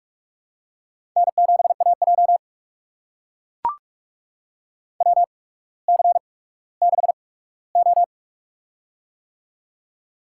Morse code, heard as a keyed tone, about 44 wpm, spaced as for 19 wpm, 700 Hz. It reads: N7AJ WC6O
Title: USA Call Signs - 4 Characters - 8x Character Spacing - 45wpm